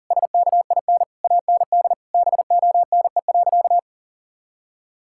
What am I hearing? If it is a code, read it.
Morse code, heard as a keyed tone, about 40 wpm, 700 Hz. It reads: SKIN AND BONE.
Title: Sentences from Top 1600 Words 40wpm